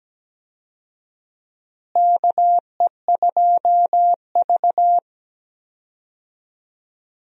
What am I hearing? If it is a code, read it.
Morse code, heard as a keyed tone, about 17 wpm, 700 Hz. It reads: KE2V
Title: USA Call Signs - 4 Characters - Repeated 2x - 17wpm